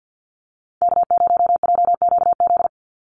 Morse code, heard as a keyed tone, about 48 wpm, 700 Hz. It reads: K0JQZ
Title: NA SOTA Activator Callsigns 50wpm